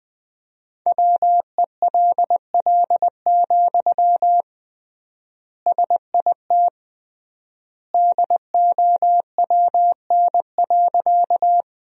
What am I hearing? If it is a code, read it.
Morse code, heard as a keyed tone, about 20 wpm, 700 Hz. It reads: WELL, SIT DOWN.
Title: 3x Word-Spacing - Sentences from Top 500 Words - 20wpm